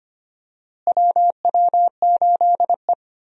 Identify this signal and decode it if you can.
Morse code, heard as a keyed tone, about 25 wpm, 700 Hz. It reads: WW8E
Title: USA Call Signs - 4 Characters - Repeated 3x - 25wpm